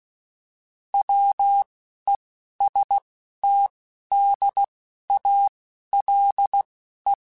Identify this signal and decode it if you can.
Morse code, heard as a keyed tone, about 16 wpm, spaced as for 11 wpm, 800 Hz. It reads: WESTDALE